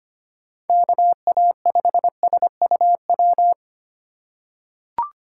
Morse code, heard as a keyed tone, about 25 wpm, 700 Hz. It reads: KA5SUW
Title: USA Call Signs - 6 Characters - 2x Character Spacing - 25wpm